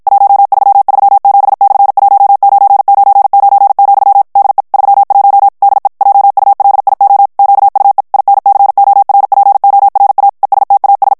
Morse code, heard as a keyed tone, about 53 wpm, 800 Hz. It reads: 1337/1999, DE 31 DE JULIO QUE INCORPORA ESTAS
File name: morse mp3 / ESPAGNOL TEXTE EN CLAIR 2 VITESSE 50 MOTS MINUTES